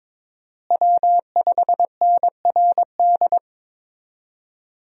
Morse code, heard as a keyed tone, about 22 wpm, 700 Hz. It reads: W5NRD
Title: USA Call Signs - 5 Characters - Repeated 2x - 22wpm